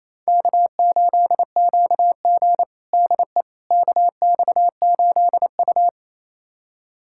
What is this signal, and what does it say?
Morse code, heard as a keyed tone, about 28 wpm, 700 Hz. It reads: K8QG DE KX8U
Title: USA Callsign DE Callsign - 4 Characters - 28wpm